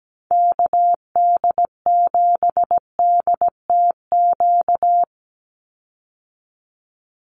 Morse code, heard as a keyed tone, about 17 wpm, 700 Hz. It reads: KD7DTQ
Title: USA Call Signs 17wpm